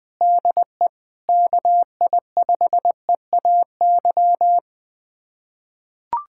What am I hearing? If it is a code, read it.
Morse code, heard as a keyed tone, about 20 wpm, 700 Hz. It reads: DE KI5EAY
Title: USA Callsign DE Callsign - 6 Characters - 20wpm